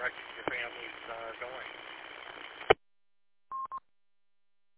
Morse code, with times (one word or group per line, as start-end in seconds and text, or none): none